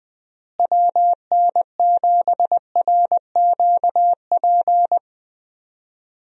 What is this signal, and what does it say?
Morse code, heard as a keyed tone, about 21 wpm, 700 Hz. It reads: WN7RQP
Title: USA Call Signs - 6 Characters - Repeated 2x - 20wpm